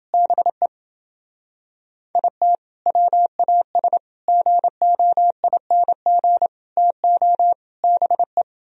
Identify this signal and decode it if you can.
Morse code, heard as a keyed tone, about 30 wpm, 700 Hz. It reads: BE IT WAS GOING TO BE